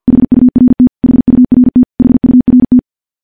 Morse code, heard as a keyed tone, about 50 wpm, 250 Hz. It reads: HURT HURT HURT
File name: morse / morse-training-50WPM